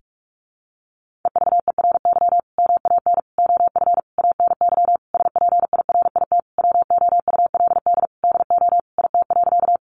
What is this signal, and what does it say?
Morse code, heard as a keyed tone, about 44 wpm, 700 Hz. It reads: EVERY MAN OF ANY SPIRIT WOULD DO IT.
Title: Sentences from Top 1400 Words 45wpm